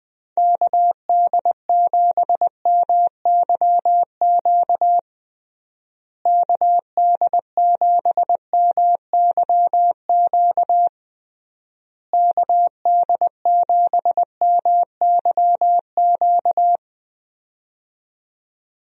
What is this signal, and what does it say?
Morse code, heard as a keyed tone, about 20 wpm, 700 Hz. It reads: KD7MYQ KD7MYQ KD7MYQ
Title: USA Call Signs - 6 Characters - Repeated 3x - 20wpm